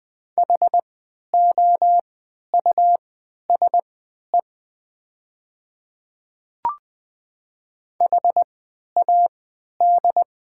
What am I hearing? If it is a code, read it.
Morse code, heard as a keyed tone, about 20 wpm, spaced as for 11 wpm, 700 Hz. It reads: HOUSE HAD
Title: 3x Character Spacing - Top 300 Words - 20wpm